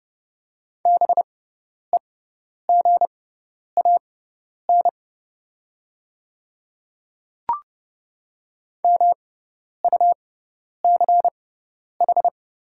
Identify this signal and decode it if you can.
Morse code, heard as a keyed tone, about 30 wpm, spaced as for 10 wpm, 690 Hz. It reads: BEGAN MUCH
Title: 6x Character Spacing - Top 500 Words - 30wpm